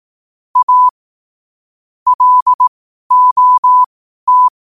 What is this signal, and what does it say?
Morse code, heard as a keyed tone, about 18 wpm, spaced as for 12 wpm, 1000 Hz. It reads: A LOT
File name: qotdfiles / 12wpm